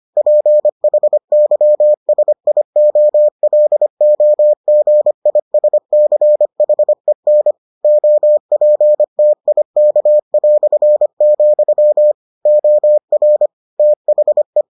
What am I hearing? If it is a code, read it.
Morse code, heard as a keyed tone, about 25 wpm, 600 Hz. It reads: PHYSIOLOGISCHEN OPTIK", OR THE